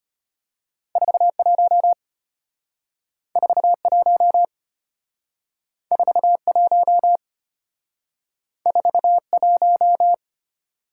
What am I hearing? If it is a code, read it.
Morse code, heard as a keyed tone, about 34 wpm, 700 Hz. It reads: 41 41 41 41